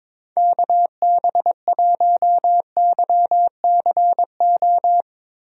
Morse code, heard as a keyed tone, about 22 wpm, 700 Hz. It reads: KB1YCO
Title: USA Call Signs - 6 Characters - Repeated 2x - 22wpm